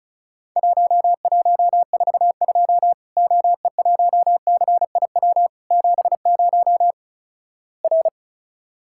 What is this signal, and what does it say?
Morse code, heard as a keyed tone, about 36 wpm, 700 Hz. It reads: 1142 OE1CIW 70 R
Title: Worked All Europe DX Contest - QTC Exchange - 35wpm